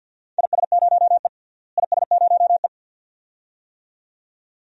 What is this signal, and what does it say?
Morse code, heard as a keyed tone, about 49 wpm, 700 Hz. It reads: IS0E IS0E
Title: DXCC-100 Call Signs - 4 Characters - Repeated 2x - 50wpm